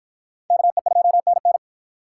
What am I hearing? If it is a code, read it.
Morse code, heard as a keyed tone, about 47 wpm, 700 Hz. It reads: KE2NN